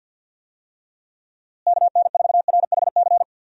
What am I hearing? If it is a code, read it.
Morse code, heard as a keyed tone, about 47 wpm, 700 Hz. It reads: KN4RHC
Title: USA Call Signs - 6 Characters - Repeated 3x - 50wpm